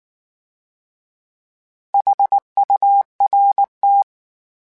Morse code, heard as a keyed tone, about 19 wpm, 800 Hz. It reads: HURT
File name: H Prefixes / H mp3s